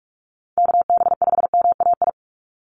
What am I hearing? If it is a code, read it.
Morse code, heard as a keyed tone, about 44 wpm, 700 Hz. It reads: KB5MAI
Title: USA Call Signs 45wpm